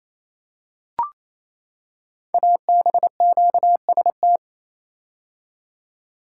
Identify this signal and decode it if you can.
Morse code, heard as a keyed tone, about 28 wpm, 700 Hz. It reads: ABQST
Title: Code Groups - 5 Letters - 28wpm